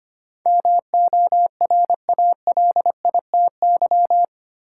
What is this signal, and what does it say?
Morse code, heard as a keyed tone, about 25 wpm, 690 Hz. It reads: MORALITY
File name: 8 Letter Words - First 3 Letters Given - 25wpm